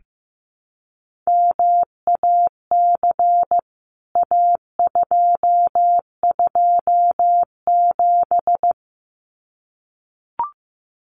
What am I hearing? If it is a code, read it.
Morse code, heard as a keyed tone, about 15 wpm, 700 Hz. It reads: MAC A227
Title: CWT Exchange 15wpm